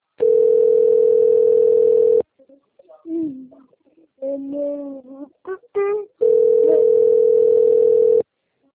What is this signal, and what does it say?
Ringback tone (North America), 440 + 480 Hz, with speech.
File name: rg-1600-unknown- - - .45069